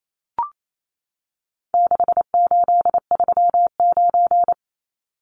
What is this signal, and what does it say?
Morse code, heard as a keyed tone, about 28 wpm, 700 Hz. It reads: 6839
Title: Numbers - 4-Digit 28wpm